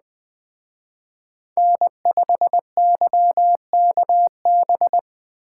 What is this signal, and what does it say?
Morse code, heard as a keyed tone, about 20 wpm, 700 Hz. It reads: N5YKB
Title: Speed Racing - USA Call Signs 17wpm